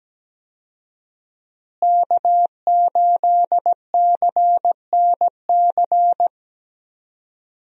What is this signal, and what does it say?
Morse code, heard as a keyed tone, about 17 wpm, 700 Hz. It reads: K8CNC
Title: USA Call Signs - 5 Characters - Repeated 3x - 17wpm